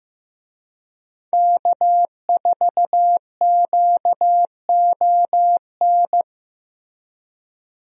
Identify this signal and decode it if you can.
Morse code, heard as a keyed tone, about 15 wpm, 700 Hz. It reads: K4QON
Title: USA Call Signs - 5 Characters - Repeated 3x - 15wpm